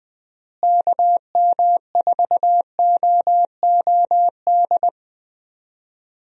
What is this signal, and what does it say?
Morse code, heard as a keyed tone, about 20 wpm, 700 Hz. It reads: KM4OOD
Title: USA Call Signs - 6 Characters - Repeated 3x - 20wpm